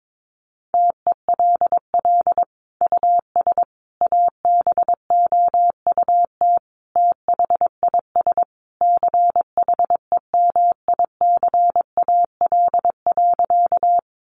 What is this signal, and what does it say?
Morse code, heard as a keyed tone, about 22 wpm, 700 Hz. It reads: TELL US ABOUT THIS CHEMICAL.
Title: Sentences from Top 1200 Words 22wpm